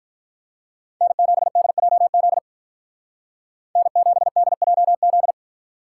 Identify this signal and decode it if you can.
Morse code, heard as a keyed tone, about 46 wpm, 700 Hz. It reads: N7DJZ N7DJZ